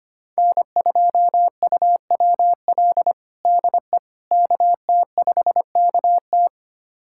Morse code, heard as a keyed tone, about 25 wpm, 700 Hz. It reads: N2UWL DE KT5KT
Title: USA Callsign DE Callsign - 5 Characters - 25wpm